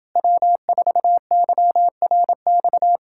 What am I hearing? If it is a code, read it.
Morse code, heard as a keyed tone, about 31 wpm, 700 Hz. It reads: W4YRX